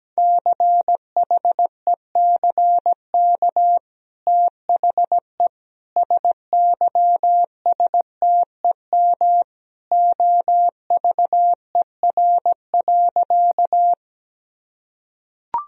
Morse code, heard as a keyed tone, about 17 wpm, 700 Hz. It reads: CHECK THE SYSTEM OVER.
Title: Sentences from Top 1000 Words - Review Part 3 - 17wpm